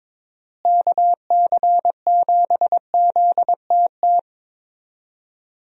Morse code, heard as a keyed tone, about 22 wpm, 700 Hz. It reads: KC7ZTT
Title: USA Call Signs - 6 Characters - Repeated 3x - 22wpm